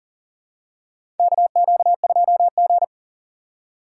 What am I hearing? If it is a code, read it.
Morse code, heard as a keyed tone, about 40 wpm, 700 Hz. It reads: KQ2G
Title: USA Call Signs - 4 Characters - 6x Character Spacing - 40wpm